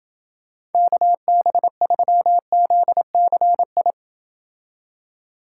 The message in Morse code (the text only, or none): KB3ZCI